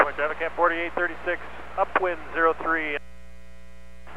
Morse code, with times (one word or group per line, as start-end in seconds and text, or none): none